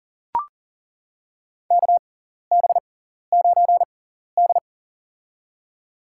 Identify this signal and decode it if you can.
Morse code, heard as a keyed tone, about 40 wpm, spaced as for 14 wpm, 700 Hz. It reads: KB9D
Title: USA Call Signs - 4 Characters - 6x Character Spacing - 40wpm